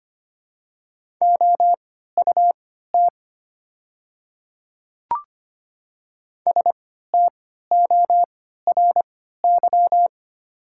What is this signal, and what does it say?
Morse code, heard as a keyed tone, about 25 wpm, spaced as for 14 wpm, 700 Hz. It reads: OUT STORY
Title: 3x Character Spacing - Top 300 Words - 25wpm